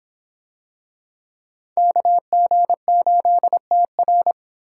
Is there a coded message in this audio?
Yes: Morse code from 1.8 to 4.3 s, heard as a keyed tone, about 26 wpm, 700 Hz. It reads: KG8TR